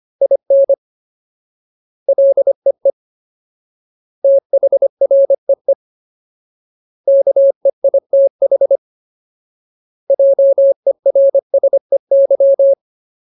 Morse code, heard as a keyed tone, about 25 wpm, 550 Hz. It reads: IN LEE THREE KEITH JERSEY
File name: Ultimate Word List / 25WPM Ultimate